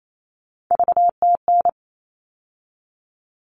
Morse code, heard as a keyed tone, about 28 wpm, 700 Hz. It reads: VTN
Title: All Cut Numbers - 3-Digit 28wpm